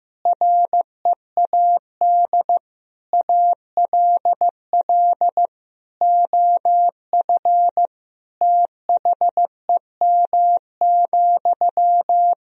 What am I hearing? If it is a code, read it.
Morse code, heard as a keyed tone, about 15 wpm, 700 Hz. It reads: READ ALL OF THEM,
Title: Sentences from Top 1000 Words - Review Part 4 - 15wpm